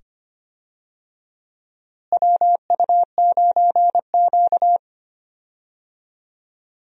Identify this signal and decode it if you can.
Morse code, heard as a keyed tone, about 25 wpm, 700 Hz. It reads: WU9Q
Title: USA Call Signs - 4 Characters - Repeated 2x - 25wpm